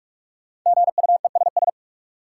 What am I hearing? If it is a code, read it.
Morse code, heard as a keyed tone, about 46 wpm, 700 Hz. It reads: GUESS
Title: Sentences from Top 1500 Words - New Word Drill - 45wpm